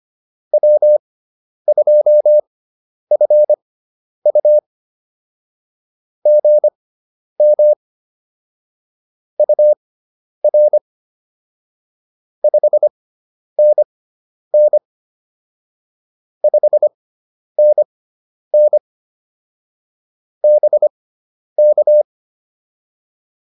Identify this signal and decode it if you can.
Morse code, heard as a keyed tone, about 25 wpm, spaced as for 10 wpm, 600 Hz. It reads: W2FU GM UR 5NN 5NN BK
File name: POTA105 10